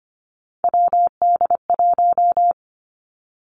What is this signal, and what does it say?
Morse code, heard as a keyed tone, about 25 wpm, 700 Hz. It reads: WD1
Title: Three Challenging Letter-Number Combinations 25wpm